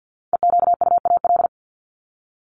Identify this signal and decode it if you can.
Morse code, heard as a keyed tone, about 48 wpm, 700 Hz. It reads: EQUAL
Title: LY Suffix - Top 100 Words with Base Words 50wpm